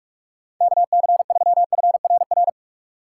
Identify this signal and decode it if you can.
Morse code, heard as a keyed tone, about 46 wpm, 700 Hz. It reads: KC3FRR
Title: USA Call Signs - 6 Characters - 3x Character Spacing - 45wpm